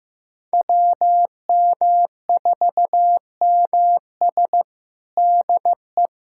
Morse code, heard as a keyed tone, about 15 wpm, 700 Hz. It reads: WM4MS DE
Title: USA Callsign DE Callsign 15wpm